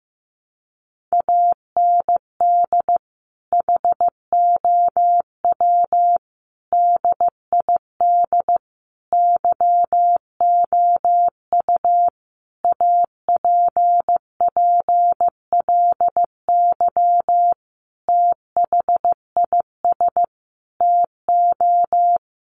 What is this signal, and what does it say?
Morse code, heard as a keyed tone, about 15 wpm, 700 Hz. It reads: AND HOW DID YOU APPLY THIS TO
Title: Sentences from Top 1200 Words 15wpm